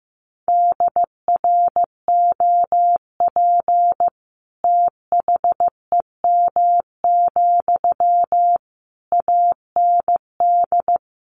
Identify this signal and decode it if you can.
Morse code, heard as a keyed tone, about 15 wpm, 700 Hz. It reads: DROP THEM, AND
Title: Sentences from Top 1400 Words 15wpm